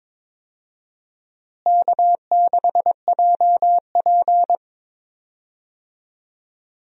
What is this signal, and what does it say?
Morse code, heard as a keyed tone, about 21 wpm, 700 Hz. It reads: K6JP
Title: USA Call Signs - 4 Characters - Repeated 3x - 22wpm